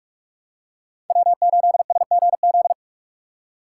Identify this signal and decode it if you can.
Morse code, heard as a keyed tone, about 48 wpm, 700 Hz. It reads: W8SGZ
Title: USA Call Signs - 5 Characters - 2x Character Spacing - 45wpm